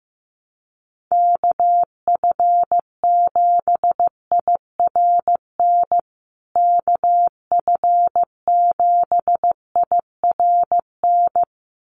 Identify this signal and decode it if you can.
Morse code, heard as a keyed tone, about 15 wpm, 700 Hz. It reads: KF7IRN KF7IRN
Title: USA Call Signs Repeated 2x 15wpm